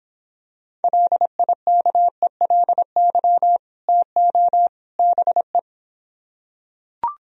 Morse code, heard as a keyed tone, about 26 wpm, 700 Hz. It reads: LIKELY TO BE